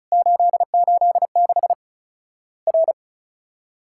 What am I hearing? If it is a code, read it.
Morse code, heard as a keyed tone, about 36 wpm, 700 Hz. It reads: 886 R
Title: Worked All Europe DX Contest - QTC Exchange - 35wpm